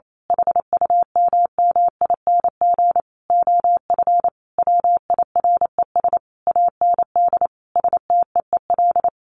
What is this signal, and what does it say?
Morse code, heard as a keyed tone, about 28 wpm, 700 Hz. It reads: HUMMING OF WIRES AND STEEL
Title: A Sound Of Thunder 28wpm